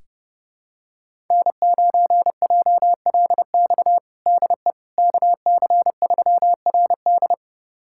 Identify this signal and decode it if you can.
Morse code, heard as a keyed tone, about 30 wpm, 700 Hz. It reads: N9JLX DE KC3RD
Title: USA Callsign DE Callsign 30wpm